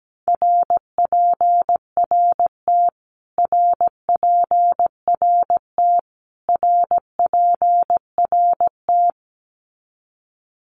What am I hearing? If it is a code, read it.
Morse code, heard as a keyed tone, about 17 wpm, 700 Hz. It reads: RPRT RPRT RPRT
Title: Abbreviations Repeated 3x - 17wpm